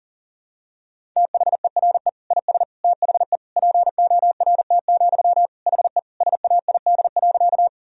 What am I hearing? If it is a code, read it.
Morse code, heard as a keyed tone, about 37 wpm, 700 Hz. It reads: THERE IS THE PORT, HE SAID.
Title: Sentences from Top 1000 Words - Review Part 1 - 40wpm